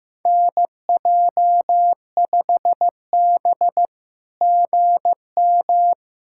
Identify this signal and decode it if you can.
Morse code, heard as a keyed tone, about 15 wpm, 700 Hz. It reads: NJ5B GM
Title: SOTA QSO Practice 15wpm